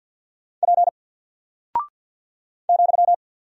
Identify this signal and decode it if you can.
Morse code, heard as a keyed tone, about 48 wpm, 700 Hz. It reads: P ,
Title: Single Character (Letter:Number:Punctuation:Prosign) Alt 50wpm